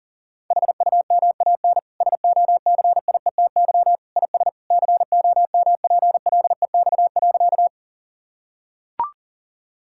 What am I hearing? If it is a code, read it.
Morse code, heard as a keyed tone, about 40 wpm, 700 Hz. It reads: HUMAN SOCIETY IS COMPLEX.
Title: Sentences from Top 1000 Words - Review Part 6 - 40wpm